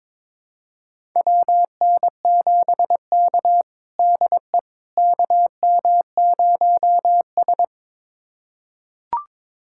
Morse code, heard as a keyed tone, about 22 wpm, 700 Hz. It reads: WN7K DE KM0S
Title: USA Callsign DE Callsign - 4 Characters - 22wpm